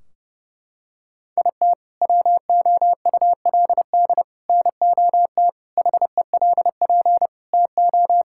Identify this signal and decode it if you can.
Morse code, heard as a keyed tone, about 30 wpm, 700 Hz. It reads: IT WOULD NOT HELP TO
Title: Sentences from Top 1600 Words 30wpm